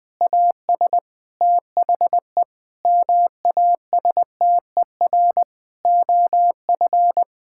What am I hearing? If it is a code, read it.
Morse code, heard as a keyed tone, about 20 wpm, 700 Hz. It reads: AS THE MASTER OF
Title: Sentences with 6-Word Phrases - 20wpm